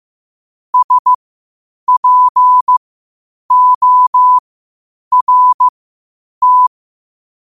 Morse code, heard as a keyed tone, about 15 wpm, spaced as for 8 wpm, 1000 Hz. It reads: SPORT
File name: qotdfiles / 8wpm